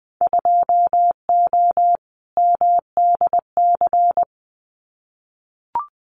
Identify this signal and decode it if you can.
Morse code, heard as a keyed tone, about 20 wpm, 700 Hz. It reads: 2O MDC
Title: Winter Field Day Exchanges 20wpm